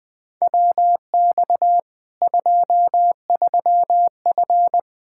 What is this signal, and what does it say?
Morse code, heard as a keyed tone, about 21 wpm, 700 Hz. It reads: WX 23F
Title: Practice QSO - Exchange 2 - Info - 20wpm